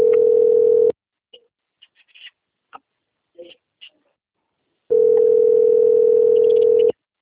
Ringback tone (North America), 440 + 480 Hz.
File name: rg-1600-unknown- - - .6010